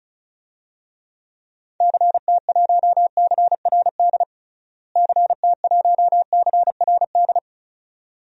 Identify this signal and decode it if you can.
Morse code, heard as a keyed tone, about 34 wpm, 700 Hz. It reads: CT1CRD CT1CRD
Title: DXCC-100 Call Signs - 6 Characters - Repeated 2x - 35wpm